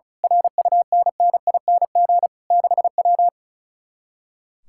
Morse code, heard as a keyed tone, about 33 wpm, 700 Hz. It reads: RUNNING 6W
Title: Speed-Racing Common QSO Phrases - 35wpm